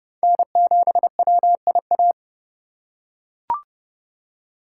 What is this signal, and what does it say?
Morse code, heard as a keyed tone, about 30 wpm, 700 Hz. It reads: N7WIA
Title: USA Call Signs - 5 Characters - 3x Character Spacing - 30wpm